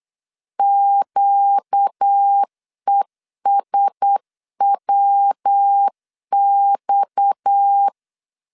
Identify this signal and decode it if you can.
Morse code, heard as a keyed tone, about 8 wpm, 800 Hz. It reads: QESWX
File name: LEZIONI / 6 PKX0 / 6 PKX 8wpm 3041